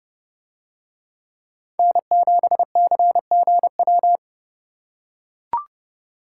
Morse code, heard as a keyed tone, about 29 wpm, 700 Hz. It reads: N7CGW